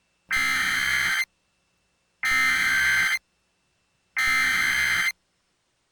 Emergency Alert System (SAME) tones (header bursts), heard as harsh buzzing bursts. It is heard from 0.3 to 5.1 s.